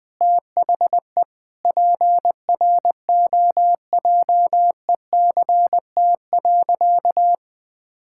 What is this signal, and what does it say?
Morse code, heard as a keyed tone, about 20 wpm, 700 Hz. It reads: THE PROJECT.
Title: Sentences with 4-Word Phrases - Spoken Beginning - 20wpm